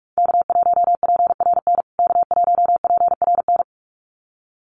Morse code, heard as a keyed tone, about 44 wpm, 700 Hz. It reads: K1PRN K1PRN
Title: USA Call Signs Repeated 2x 45wpm